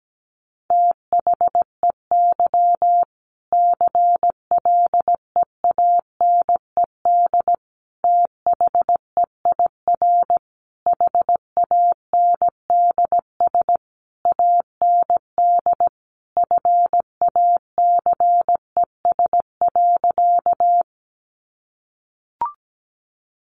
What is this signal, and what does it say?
Morse code, heard as a keyed tone, about 17 wpm, 700 Hz. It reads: THEY CLEANED THEIR HANDS AND FACES.
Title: A Sound Of Thunder 17wpm